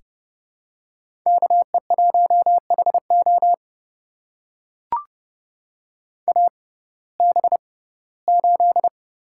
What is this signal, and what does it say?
Morse code, heard as a keyed tone, about 30 wpm, spaced as for 18 wpm, 700 Hz. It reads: KE1HO AB8
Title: USA Call Signs - 5 Characters - 6x Character Spacing - 30wpm